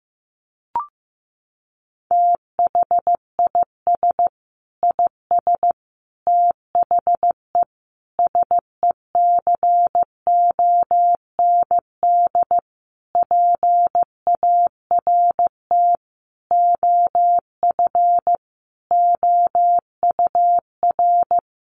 Morse code, heard as a keyed tone, about 15 wpm, 700 Hz. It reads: THIS IS THE SECOND PART OF OUR
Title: Sentences from Top 1400 Words 15wpm